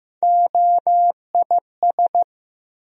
Morse code, heard as a keyed tone, about 15 wpm, 700 Hz. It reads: OIS
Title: USA Call Sign Postfixes - 3 Letters 15wpm